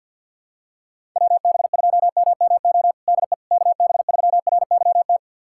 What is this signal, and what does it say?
Morse code, heard as a keyed tone, about 50 wpm, 700 Hz. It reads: WB2MMO DE KB3RYT
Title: USA Callsign DE Callsign - 6 Characters - 50wpm